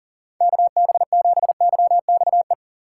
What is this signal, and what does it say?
Morse code, heard as a keyed tone, about 39 wpm, 700 Hz. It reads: KB7YXE
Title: USA Call Signs - 6 Characters - 2x Character Spacing - 40wpm